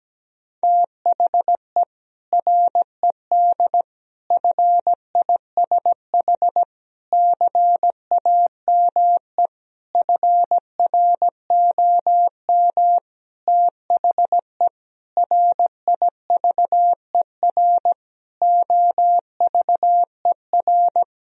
Morse code, heard as a keyed tone, about 17 wpm, 700 Hz. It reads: THE RED FISH CAME FROM THE RIVER OVER
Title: Sentences from Top 1000 Words - Review Part 2 - 17wpm